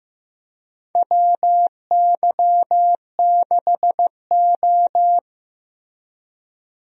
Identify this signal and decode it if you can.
Morse code, heard as a keyed tone, about 15 wpm, 700 Hz. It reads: WY6O
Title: USA Call Signs - 4 Characters - Repeated 2x - 15wpm